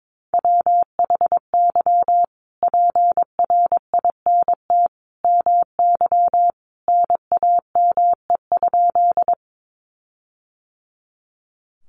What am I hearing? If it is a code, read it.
Morse code, heard as a keyed tone, about 22 wpm, 700 Hz. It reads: WHY PRINT MY NAME?
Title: Sentences from Top 1400 Words 22wpm